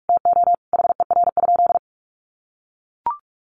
Morse code, heard as a keyed tone, about 44 wpm, 700 Hz. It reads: TO HER?
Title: Sentences from Top 1400 Words 45wpm